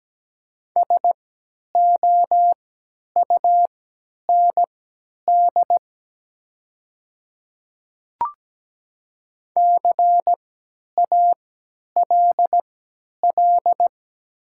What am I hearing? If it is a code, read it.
Morse code, heard as a keyed tone, about 17 wpm, spaced as for 10 wpm, 700 Hz. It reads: SOUND CALL
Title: 3x Character Spacing - Top 300 Words - 17wpm